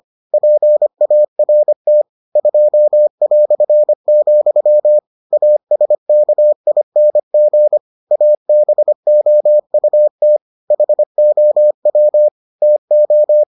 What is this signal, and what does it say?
Morse code, heard as a keyed tone, about 25 wpm, 600 Hz. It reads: PART 2", ASKING ABOUT HOW TO